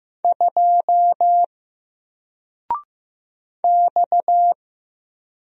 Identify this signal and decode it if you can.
Morse code, heard as a keyed tone, about 15 wpm, 700 Hz. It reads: 2 X
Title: Single Character (Letter:Number:Punctuation:Prosign) Alt 15wpm